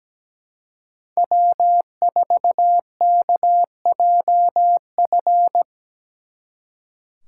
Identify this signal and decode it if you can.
Morse code, heard as a keyed tone, about 20 wpm, 700 Hz. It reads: W4KJF
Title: Speed Racing - USA Call Signs 17wpm